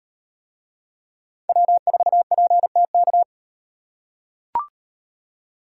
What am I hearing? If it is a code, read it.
Morse code, heard as a keyed tone, about 36 wpm, 700 Hz. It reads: W4PTK